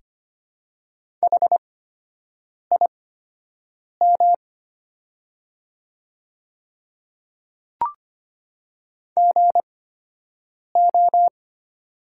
Morse code, heard as a keyed tone, about 25 wpm, spaced as for 7 wpm, 700 Hz. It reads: HIM GO
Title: 8x Character Spacing - Top 100 Words - 25wpm